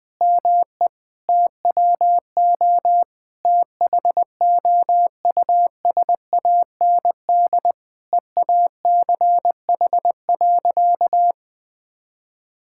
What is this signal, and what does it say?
Morse code, heard as a keyed tone, about 21 wpm, 700 Hz. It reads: ME TWO THOUSAND EACH.
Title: Sentences from Top 1000 Words - Review Part 5 - 20wpm